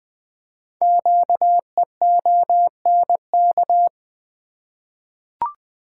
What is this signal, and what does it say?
Morse code, heard as a keyed tone, about 20 wpm, 700 Hz. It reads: QEONK
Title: Code Groups - 5 Letters - 20wpm